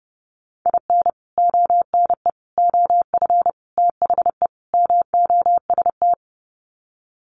Morse code, heard as a keyed tone, about 30 wpm, 700 Hz. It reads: IN ONE OF THE MOST
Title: Sets of 5 Words Encore 30wpm